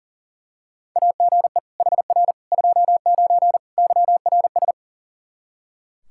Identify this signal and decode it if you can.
Morse code, heard as a keyed tone, about 40 wpm, 690 Hz. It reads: AGE HR 29 YRS
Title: Practice QSO - Exchange 3 - More Info - 40wpm